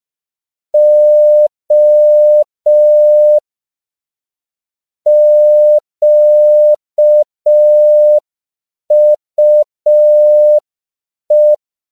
Morse code, heard as a keyed tone, about 5 wpm, 600 Hz. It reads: O QUE
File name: CW / 5PPM